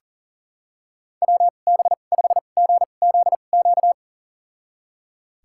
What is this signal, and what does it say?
Morse code, heard as a keyed tone, about 40 wpm, spaced as for 29 wpm, 700 Hz. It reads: WB5GZQ
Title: USA Call Signs - 6 Characters - 2x Character Spacing - 40wpm